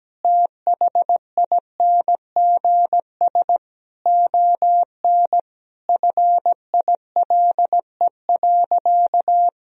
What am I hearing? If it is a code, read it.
Morse code, heard as a keyed tone, about 17 wpm, 700 Hz. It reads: THINGS ON FILE.
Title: Sentences from Top 1000 Words - Review Part 4 - 17wpm